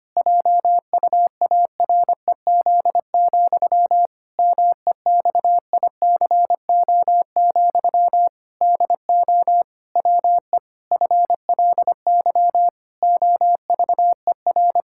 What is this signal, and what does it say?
Morse code, heard as a keyed tone, about 25 wpm, 700 Hz. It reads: JUAREZ, MEXICO, DO WE FLY OVER
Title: Pangram Challenge v2 - 25wpm